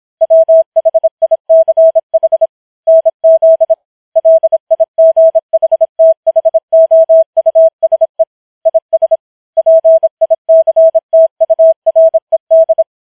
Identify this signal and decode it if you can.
Morse code, heard as a keyed tone, about 26 wpm, 650 Hz. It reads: WHICH NZ LIGHTHOUSE IS PICTURED
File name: challenge191